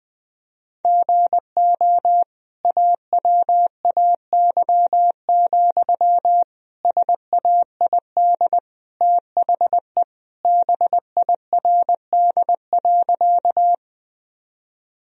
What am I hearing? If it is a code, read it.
Morse code, heard as a keyed tone, about 21 wpm, 700 Hz. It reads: GO AWAY, SAID THE BIRD.
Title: Sentences from Top 1000 Words - Review Part 2 - 20wpm